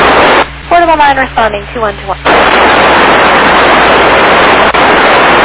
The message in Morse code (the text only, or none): none